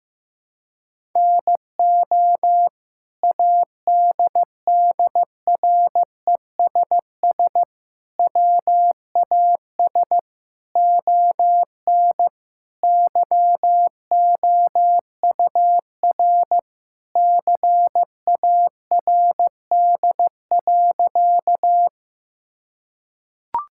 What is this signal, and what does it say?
Morse code, heard as a keyed tone, about 14 wpm, 700 Hz. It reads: NO ADDRESS WAS ON YOUR CARD.
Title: Sentences from Top 1000 Words - Review Part 6 - 15wpm